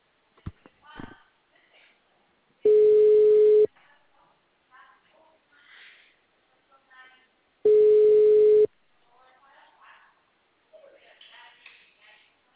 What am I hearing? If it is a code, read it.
Ringback tone (Europe, 425 Hz), with speech.
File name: out- -2007- - - .610